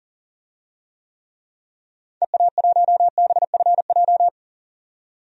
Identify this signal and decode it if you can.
Morse code, heard as a keyed tone, about 37 wpm, 700 Hz. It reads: EA1BFJ
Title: DXCC-100 Call Signs - 6 Characters - 40wpm